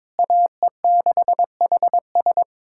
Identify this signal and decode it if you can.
Morse code, heard as a keyed tone, about 22 wpm, 700 Hz. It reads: AE6HS
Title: USA Callsign DE Callsign - 5 Characters - 22wpm